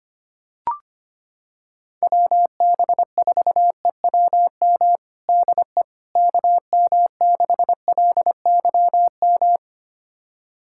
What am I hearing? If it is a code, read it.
Morse code, heard as a keyed tone, about 25 wpm, 700 Hz. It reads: WB4EWM DE KM6LYM
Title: USA Callsign DE Callsign 25wpm